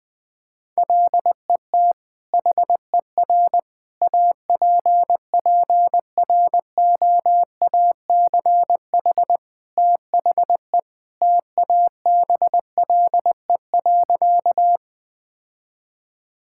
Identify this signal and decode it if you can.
Morse code, heard as a keyed tone, about 21 wpm, 700 Hz. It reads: LET HER APPROACH THE TABLE.
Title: Sentences from Top 1000 Words - Review Part 6 - 20wpm